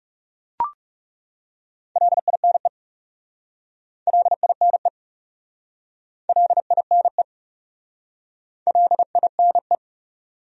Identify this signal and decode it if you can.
Morse code, heard as a keyed tone, about 40 wpm, 700 Hz. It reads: LINE LINE LINE LINE